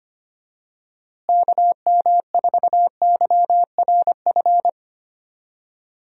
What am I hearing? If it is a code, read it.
Morse code, heard as a keyed tone, about 25 wpm, 700 Hz. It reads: KM4YRF
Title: USA Call Signs - 6 Characters - Repeated 3x - 25wpm